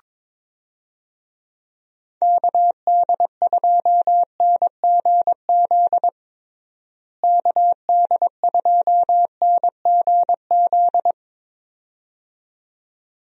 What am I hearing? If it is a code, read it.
Morse code, heard as a keyed tone, about 22 wpm, 700 Hz. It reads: KD2NGZ KD2NGZ
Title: USA Call Signs - 6 Characters - Repeated 2x - 22wpm